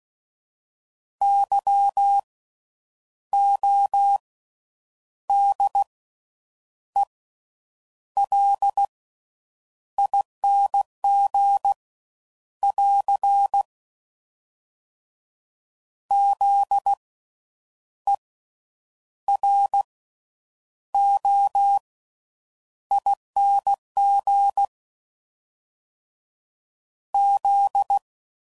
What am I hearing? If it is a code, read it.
Morse code, heard as a keyed tone, about 16 wpm, spaced as for 8 wpm, 800 Hz. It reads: YODELING <AR> ZEROING Z
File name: Slow With Faster Ings